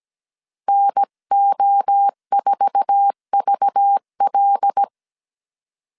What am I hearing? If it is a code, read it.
Morse code, heard as a keyed tone, about 17 wpm, 800 Hz. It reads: NO4VL
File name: CW / DET / 1 9 1500C S C 16wpm 2909